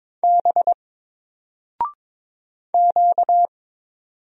Morse code, heard as a keyed tone, about 22 wpm, 700 Hz. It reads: B Q
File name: Single-Letter-Number-22wpm